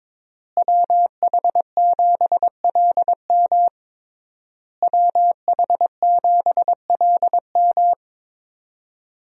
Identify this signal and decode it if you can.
Morse code, heard as a keyed tone, about 22 wpm, 700 Hz. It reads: WH7LM WH7LM
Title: USA Call Signs - 5 Characters - Repeated 3x - 22wpm